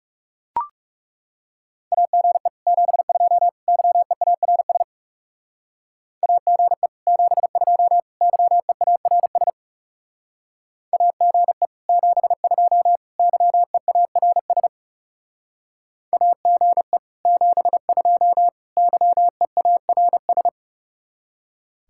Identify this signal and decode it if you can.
Morse code, heard as a keyed tone, about 36 wpm, 700 Hz. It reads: AGE 72 YEARS AGE 72 YEARS AGE 72 YEARS AGE 72 YEARS
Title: Speed-Racing Common QSO Phrases Encore - 30wpm